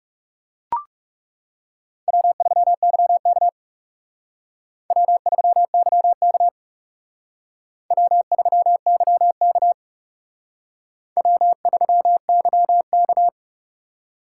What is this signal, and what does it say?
Morse code, heard as a keyed tone, about 38 wpm, 700 Hz. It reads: W3YK W3YK W3YK W3YK
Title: Speed Racing - USA Call Signs 30wpm